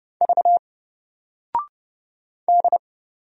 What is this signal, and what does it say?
Morse code, heard as a keyed tone, about 31 wpm, 700 Hz. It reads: V D
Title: Single Letter-Number 30wpm